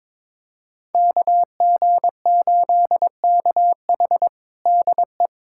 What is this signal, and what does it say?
Morse code, heard as a keyed tone, about 22 wpm, 700 Hz. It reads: KG8KH DE
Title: USA Callsign DE Callsign - 5 Characters - 22wpm